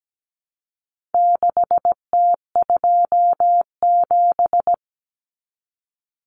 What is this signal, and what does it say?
Morse code, heard as a keyed tone, about 17 wpm, 700 Hz. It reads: 6T27
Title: Common Cut Numbers - 4-Digit 17wpm